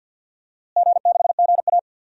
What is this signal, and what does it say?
Morse code, heard as a keyed tone, about 49 wpm, 700 Hz. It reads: G6GA
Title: DXCC-100 Call Signs - 4 Characters - Repeated 3x - 50wpm